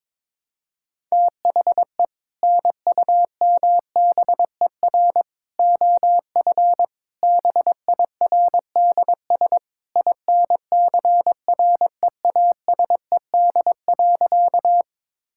Morse code, heard as a keyed tone, about 22 wpm, 700 Hz. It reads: THE NUMBER OF BIRDS INCREASED.
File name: practice / 22wpm / Sentences with 3-Word Phrases - 22wpm